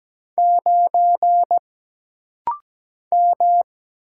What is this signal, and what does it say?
Morse code, heard as a keyed tone, about 17 wpm, 700 Hz. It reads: 9 M
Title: Single Letter-Number Alt - Rapid-Fire 17wpm